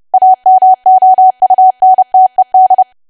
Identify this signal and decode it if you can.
Morse code, heard as a keyed tone, about 30 wpm, 750 Hz. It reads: AMOUNTED
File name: cwa / W1AW-Level3 / 30WPM